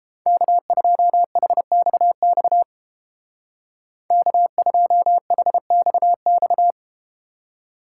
Morse code, heard as a keyed tone, about 31 wpm, 700 Hz. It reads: K2HXX K2HXX